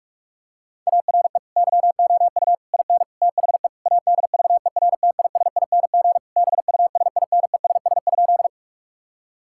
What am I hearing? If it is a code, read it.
Morse code, heard as a keyed tone, about 46 wpm, 700 Hz. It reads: ARE YOU IN THE ADVERTISING BUSINESS?
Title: Sentences from Top 1000 Words - Review Part 6 - 45wpm